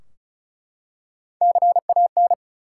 Morse code, heard as a keyed tone, about 36 wpm, 700 Hz. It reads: CAN
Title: Top 180 QSO Elements 35wpm